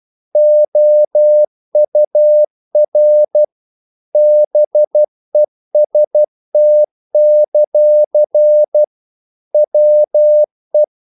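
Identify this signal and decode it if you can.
Morse code, heard as a keyed tone, about 12 wpm, 600 Hz. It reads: OUR BEST; WE
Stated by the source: Morse practice sending English quotations.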